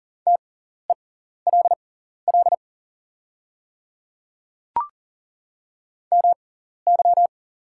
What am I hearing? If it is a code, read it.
Morse code, heard as a keyed tone, about 39 wpm, spaced as for 14 wpm, 700 Hz. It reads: TELL MY
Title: 6x Character Spacing - Top 200 Words - 40wpm